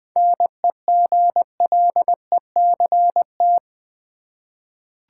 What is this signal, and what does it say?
Morse code, heard as a keyed tone, about 20 wpm, 700 Hz. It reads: NEGLECT
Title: 7 Letter Words - First 3 Letters Given - 20wpm